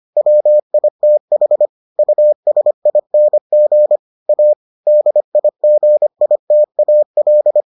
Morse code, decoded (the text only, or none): WITH USING A DIGITAL